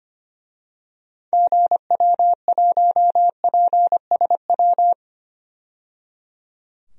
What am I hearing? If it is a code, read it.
Morse code, heard as a keyed tone, about 25 wpm, 700 Hz. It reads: GW1PSW
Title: DXCC-100 Call Signs - 6 Characters - 25wpm